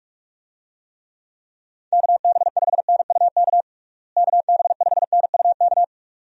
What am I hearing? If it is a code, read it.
Morse code, heard as a keyed tone, about 42 wpm, 700 Hz. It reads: KB5NUK KB5NUK
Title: USA Call Signs - 6 Characters - Repeated 2x - 45wpm